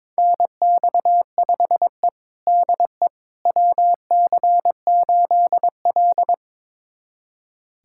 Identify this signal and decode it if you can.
Morse code, heard as a keyed tone, about 21 wpm, 700 Hz. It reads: NX5E DE WC8L
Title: USA Callsign DE Callsign - 4 Characters - 22wpm